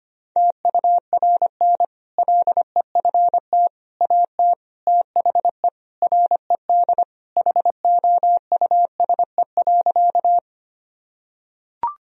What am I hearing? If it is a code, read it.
Morse code, heard as a keyed tone, about 25 wpm, 700 Hz. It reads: TURN LEFT AT THE RED HOUSE.
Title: Sentences from Top 1000 Words - Review Part 2 - 25wpm